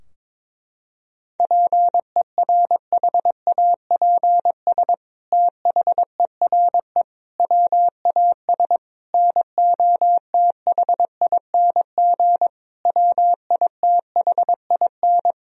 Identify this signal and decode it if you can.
Morse code, heard as a keyed tone, about 22 wpm, 700 Hz. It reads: PERHAPS THERE WAS NOTHING WITHIN
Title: Sentences from Top 1000 Words - Review Part 4 - 22wpm